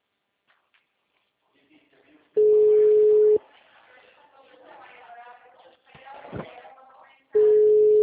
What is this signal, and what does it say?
Ringback tone (Europe, 425 Hz), with speech.